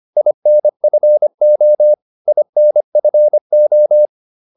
Morse code, heard as a keyed tone, about 25 wpm, 600 Hz. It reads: INFO INFO